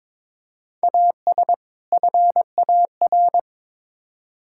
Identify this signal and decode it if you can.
Morse code, heard as a keyed tone, about 22 wpm, 700 Hz. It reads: AS FAR
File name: practice / 22wpm / Build A Phrase - 5-Word Phrases - 22wpm